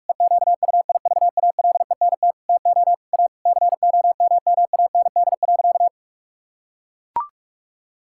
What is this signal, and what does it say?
Morse code, heard as a keyed tone, about 41 wpm, 700 Hz. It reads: EQUIVALENT TO A COMMAND.
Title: Sentences from Top 1600 Words 45wpm